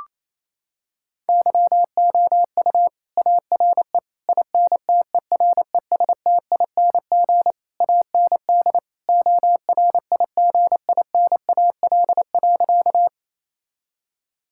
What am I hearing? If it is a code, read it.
Morse code, heard as a keyed tone, about 28 wpm, 700 Hz. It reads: YOU ARE INTERESTING AND ORIGINAL.
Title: Sentences from Top 1000 Words - Review Part 6 - 28wpm